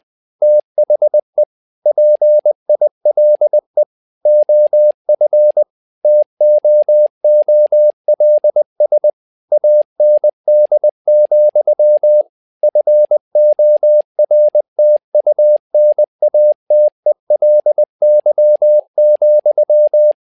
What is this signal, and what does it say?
Morse code, heard as a keyed tone, about 20 wpm, 600 Hz. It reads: THE PILE OF TOOLS AND, FORTUNATELY,
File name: tower trouble 20wpm